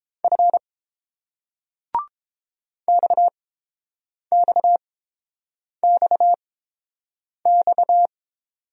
Morse code, heard as a keyed tone, about 29 wpm, spaced as for 7 wpm, 700 Hz. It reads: F XXXX